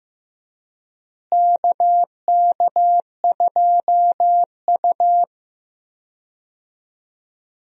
Morse code, heard as a keyed tone, about 15 wpm, 700 Hz. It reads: KK2U
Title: USA Call Signs - 4 Characters - Repeated 2x - 15wpm